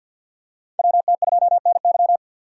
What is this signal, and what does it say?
Morse code, heard as a keyed tone, about 46 wpm, 700 Hz. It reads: WT2NY
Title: USA Call Signs - 5 Characters - 2x Character Spacing - 50wpm